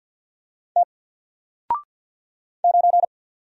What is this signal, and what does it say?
Morse code, heard as a keyed tone, about 48 wpm, 700 Hz. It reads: T 9
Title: Single Letter-Number Alt 50wpm